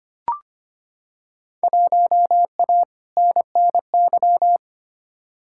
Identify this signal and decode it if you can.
Morse code, heard as a keyed tone, about 25 wpm, 700 Hz. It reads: 1A NNY
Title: Field Day Exchanges Encore - 25wpm